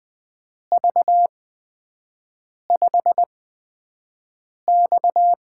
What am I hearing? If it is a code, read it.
Morse code, heard as a keyed tone, about 20 wpm, spaced as for 5 wpm, 700 Hz. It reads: V5X
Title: USA Call Signs - 8x Character Spacing - 20wpm